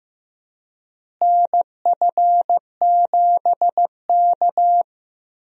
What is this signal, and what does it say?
Morse code, heard as a keyed tone, about 15 wpm, 700 Hz. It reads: NF7K
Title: USA Call Signs - 4 Characters - Repeated 2x - 15wpm